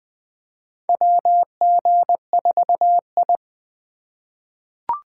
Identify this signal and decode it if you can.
Morse code, heard as a keyed tone, about 20 wpm, 700 Hz. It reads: WG4I
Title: 2023 SOTA Chaser Call Signs - 20wpm